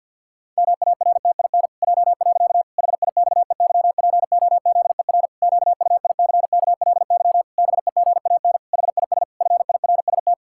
Morse code, heard as a keyed tone, about 46 wpm, 700 Hz. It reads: MARTIN J. HIXEYPOZER QUICKLY BEGAN HIS FIRST
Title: Pangram Challenge v2 - 50wpm